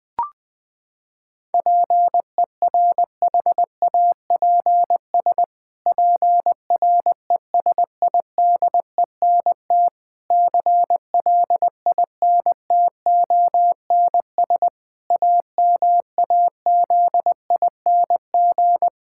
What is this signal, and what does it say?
Morse code, heard as a keyed tone, about 20 wpm, 700 Hz. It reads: PERHAPS PRESIDENT CLINTONS AMAZING
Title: Pangram Challenge v2 - 20wpm